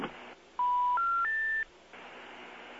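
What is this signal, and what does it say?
Special information tone.
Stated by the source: International telephone call progress tone.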